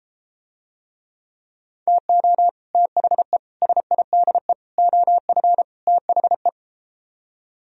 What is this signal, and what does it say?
Morse code, heard as a keyed tone, about 34 wpm, 700 Hz. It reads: TO THE SIDE OF THE